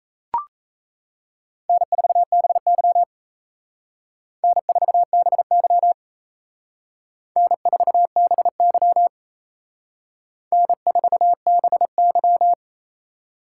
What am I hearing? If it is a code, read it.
Morse code, heard as a keyed tone, about 36 wpm, 700 Hz. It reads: N4BY N4BY N4BY N4BY